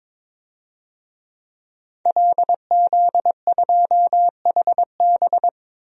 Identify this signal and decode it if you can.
Morse code, heard as a keyed tone, about 22 wpm, 700 Hz. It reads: LZ2HB
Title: DXCC-100 Call Signs - 5 Characters - 22wpm